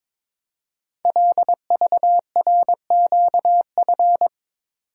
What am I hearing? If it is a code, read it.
Morse code, heard as a keyed tone, about 22 wpm, 700 Hz. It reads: LVRQF
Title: Code Groups - 5 Letters - 22wpm